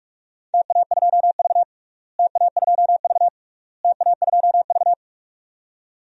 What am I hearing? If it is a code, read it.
Morse code, heard as a keyed tone, about 43 wpm, 700 Hz. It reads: TA2V TA2V TA2V
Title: DXCC-100 Call Signs - 4 Characters - Repeated 3x - 45wpm